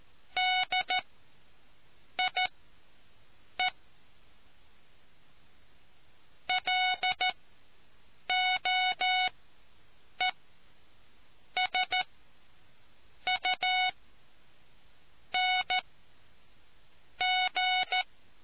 Morse code, heard as a keyed tone, about 14 wpm, spaced as for 6 wpm, 740 Hz. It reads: DIE LOESUNG